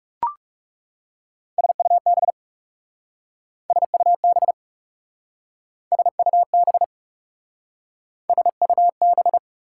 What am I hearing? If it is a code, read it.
Morse code, heard as a keyed tone, about 38 wpm, 690 Hz. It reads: SUB SUB SUB SUB